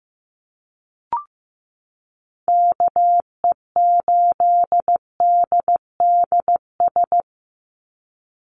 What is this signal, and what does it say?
Morse code, heard as a keyed tone, about 15 wpm, 700 Hz. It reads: KE8DDS
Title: USA Call Signs 15wpm